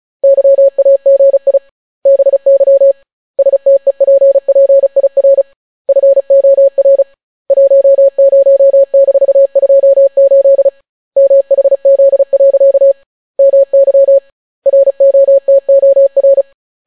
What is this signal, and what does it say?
Morse code, heard as a keyed tone, about 36 wpm, 550 Hz. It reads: YAGI BY STEPPIR FOR 10-28 MHZ. MY ROTOR